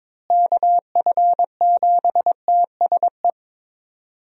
Morse code, heard as a keyed tone, about 22 wpm, 700 Hz. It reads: KF7TSE
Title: USA Call Signs - 6 Characters - Repeated 3x - 22wpm